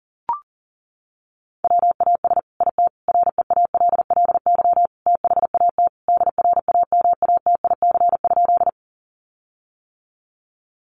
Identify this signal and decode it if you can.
Morse code, heard as a keyed tone, about 40 wpm, 700 Hz. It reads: WAS IT REALLY THAT DRAMATIC?
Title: Sentences from Top 1300 Words 40wpm